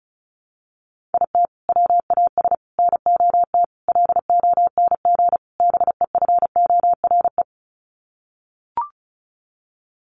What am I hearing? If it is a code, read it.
Morse code, heard as a keyed tone, about 34 wpm, 700 Hz. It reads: IT WAS NOT LONG BEFORE
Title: Sets of 5 Words Encore2 35wpm